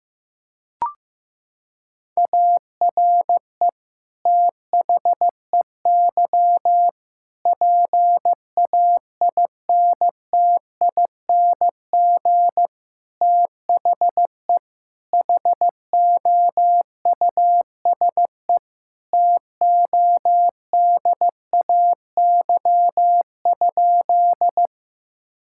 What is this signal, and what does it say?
Morse code, heard as a keyed tone, about 14 wpm, 700 Hz. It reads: ARE THEY PAINTING THE HOUSE TODAY?
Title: Sentences from Top 1000 Words - Review Part 6 - 15wpm